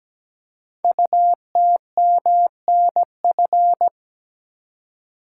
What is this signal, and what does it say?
Morse code, heard as a keyed tone, about 17 wpm, 700 Hz. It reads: UTMNF
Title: Code Groups - 5 Letters - 17wpm